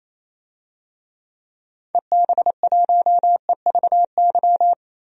Morse code, heard as a keyed tone, about 28 wpm, 700 Hz. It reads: EB1EVY
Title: DXCC-100 Call Signs - 6 Characters - Repeated 3x - 28wpm